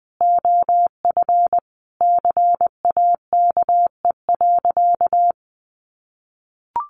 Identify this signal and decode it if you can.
Morse code, heard as a keyed tone, about 20 wpm, 700 Hz. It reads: OF CAKE.
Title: Additional Easy Phrases 20wpm